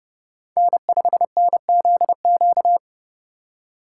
Morse code, heard as a keyed tone, about 30 wpm, 700 Hz. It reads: N5NZQ
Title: USA Call Signs - 5 Characters - 4x Character Spacing - 30wpm